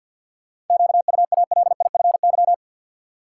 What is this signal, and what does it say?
Morse code, heard as a keyed tone, about 51 wpm, 700 Hz. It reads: QUALIFY
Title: Seven Letter Words - 50wpm